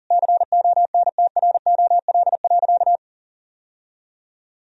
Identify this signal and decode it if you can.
Morse code, heard as a keyed tone, about 37 wpm, 700 Hz. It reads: CONTROL.
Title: Sentences from Top 1000 Words - Review Part 3 - 40wpm